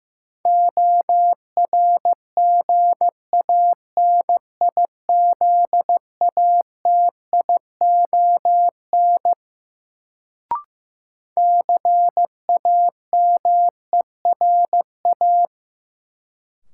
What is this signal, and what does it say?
Morse code, heard as a keyed tone, about 16 wpm, 700 Hz. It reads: ORGANIZATION CAMERA
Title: Sentences from Top 1000 Words - Review Part 5 - New Word Drill - 15wpm